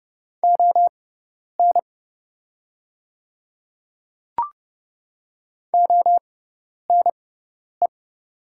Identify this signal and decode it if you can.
Morse code, heard as a keyed tone, about 30 wpm, spaced as for 10 wpm, 700 Hz. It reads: ON ONE
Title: 6x Character Spacing - Top 100 Words - 30wpm